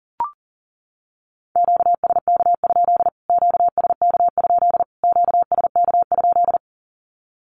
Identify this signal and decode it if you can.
Morse code, heard as a keyed tone, about 40 wpm, 700 Hz. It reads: QSK? QSK? QSK?
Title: Q-Codes Repeated 3x 40wpm